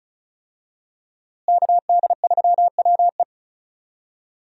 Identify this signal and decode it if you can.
Morse code, heard as a keyed tone, about 34 wpm, 700 Hz. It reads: KD3WE
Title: USA Call Signs - 5 Characters - Repeated 3x - 35wpm